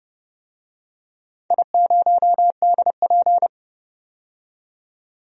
Morse code, heard as a keyed tone, about 28 wpm, 700 Hz. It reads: I0DP
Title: DXCC-100 Call Signs - 4 Characters - 30wpm